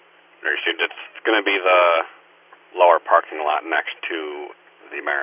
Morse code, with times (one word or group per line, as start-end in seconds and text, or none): none